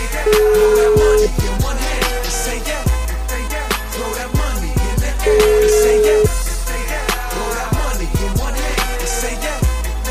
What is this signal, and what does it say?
Ringback tone (Europe, 425 Hz), with noise.